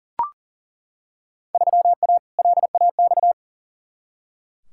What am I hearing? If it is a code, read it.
Morse code, heard as a keyed tone, about 40 wpm, 700 Hz. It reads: 3A LAX
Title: Field Day Exchanges - 40wpm